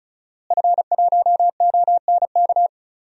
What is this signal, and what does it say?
Morse code, heard as a keyed tone, about 34 wpm, 700 Hz. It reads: F1ONK
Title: DXCC-100 Call Signs - 5 Characters - Repeated 2x - 35wpm